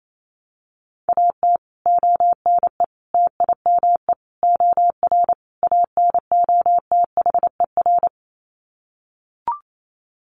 Morse code, heard as a keyed tone, about 28 wpm, 700 Hz. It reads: AT ONE TIME OR ANOTHER
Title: Sets of 5 Words Encore 28wpm